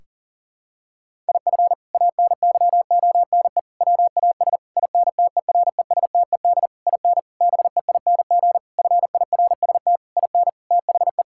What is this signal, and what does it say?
Morse code, heard as a keyed tone, about 40 wpm, 700 Hz. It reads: IF ANYONE WAS INTERESTED IN BEING FIRST IN THE
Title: Sentences from Top 1600 Words 40wpm